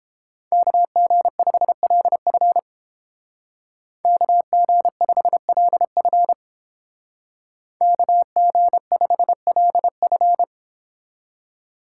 Morse code, heard as a keyed tone, about 30 wpm, 700 Hz. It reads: KG5LF KG5LF KG5LF